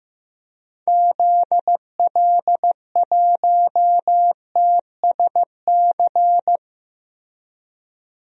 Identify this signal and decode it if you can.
Morse code, heard as a keyed tone, about 15 wpm, 700 Hz. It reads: ZL1TSC
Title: DXCC-100 Call Signs - 6 Characters - Repeated 3x - 15wpm